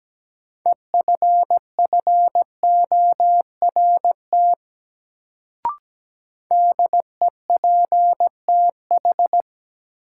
Morse code, heard as a keyed tone, about 17 wpm, 700 Hz. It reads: EFFORT DEPTH
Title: Sentences from Top 1200 Words - New Word Drill - 17wpm